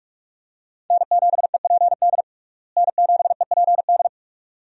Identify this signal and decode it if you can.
Morse code, heard as a keyed tone, about 46 wpm, 700 Hz. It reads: N7EPD N7EPD
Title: USA Call Signs - 5 Characters - Repeated 2x - 45wpm